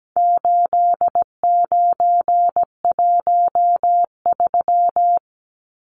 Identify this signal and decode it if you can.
Morse code, heard as a keyed tone, about 17 wpm, 700 Hz. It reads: 8913
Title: Numbers - 4-Digit 17wpm